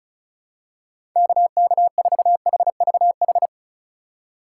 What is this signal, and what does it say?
Morse code, heard as a keyed tone, about 34 wpm, 700 Hz. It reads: KK4HVH
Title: USA Call Signs - 6 Characters - 3x Character Spacing - 35wpm